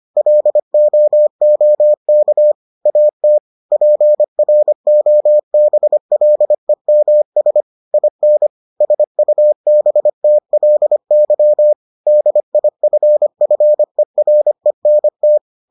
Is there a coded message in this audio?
Yes: Morse code from 0.2 to 15.4 s, heard as a keyed tone, about 25 wpm, 600 Hz. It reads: LOOK AT PROBLEMS IN SUBTLY DIFFERENT